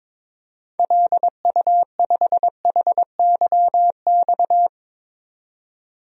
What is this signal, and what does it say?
Morse code, heard as a keyed tone, about 22 wpm, 700 Hz. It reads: LU5HYX
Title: DXCC-100 Call Signs - 6 Characters - 22wpm